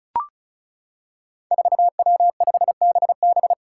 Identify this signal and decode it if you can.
Morse code, heard as a keyed tone, about 34 wpm, 700 Hz. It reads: 4W5BB
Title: Five Challenging Letter-Number Combinations 35wpm